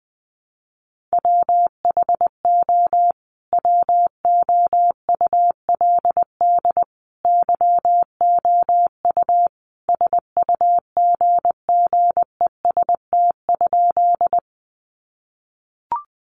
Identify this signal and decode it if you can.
Morse code, heard as a keyed tone, about 20 wpm, 700 Hz. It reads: WHO WOULD YOU SUGGEST?
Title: Sentences from Top 1300 Words 20wpm